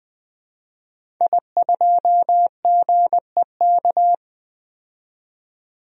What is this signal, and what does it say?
Morse code, heard as a keyed tone, about 20 wpm, 700 Hz. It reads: I2GEK
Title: DXCC-100 Call Signs - 5 Characters - 20wpm